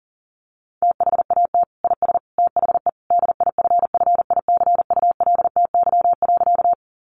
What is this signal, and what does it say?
Morse code, heard as a keyed tone, about 40 wpm, 700 Hz. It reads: THAT IS THE DIFFICULTY.
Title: Sentences from Top 1500 Words 40wpm